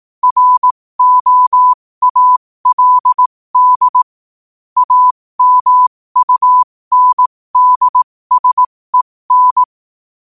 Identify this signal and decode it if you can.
Morse code, heard as a keyed tone, about 18 wpm, spaced as for 15 wpm, 1000 Hz. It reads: ROALD AMUNDSEN
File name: qotdfiles / 15wpm